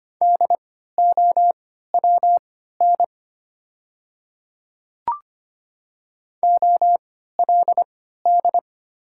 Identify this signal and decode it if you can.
Morse code, heard as a keyed tone, about 25 wpm, spaced as for 14 wpm, 700 Hz. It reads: DOWN OLD
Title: 3x Character Spacing - Top 300 Words - 25wpm